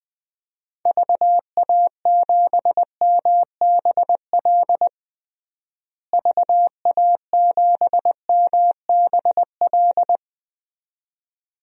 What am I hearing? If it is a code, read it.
Morse code, heard as a keyed tone, about 21 wpm, 700 Hz. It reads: VA7MBL VA7MBL
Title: DXCC-100 Call Signs - 6 Characters - Repeated 2x - 20wpm